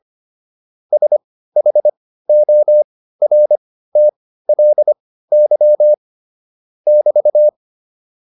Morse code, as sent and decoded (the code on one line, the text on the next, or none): ... .... --- .-. - .-.. -.-- / -...-
SHORTLY <BT>